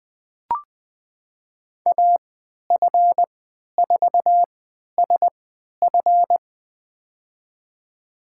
Morse code, heard as a keyed tone, about 20 wpm, spaced as for 11 wpm, 700 Hz. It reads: AF4SF
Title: USA Call Signs - 5 Characters - 3x Character Spacing - 20wpm